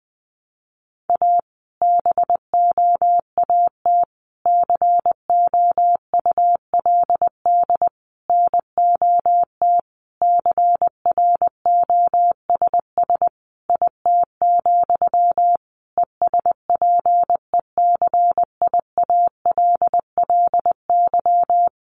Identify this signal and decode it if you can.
Morse code, heard as a keyed tone, about 20 wpm, 700 Hz. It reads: A BOAT COULD NOT CROSS IT, ESPECIALLY
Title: Sentences from Top 1400 Words 20wpm